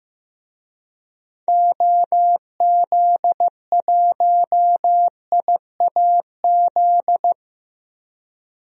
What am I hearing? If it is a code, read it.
Morse code, heard as a keyed tone, about 15 wpm, 700 Hz. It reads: OZ1IAZ
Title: DXCC-100 Call Signs - 6 Characters - Repeated 2x - 15wpm